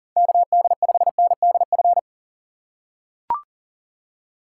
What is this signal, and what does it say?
Morse code, heard as a keyed tone, about 40 wpm, 700 Hz. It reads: KD5NDF
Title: USA Call Signs - 6 Characters - 4x Character Spacing - 40wpm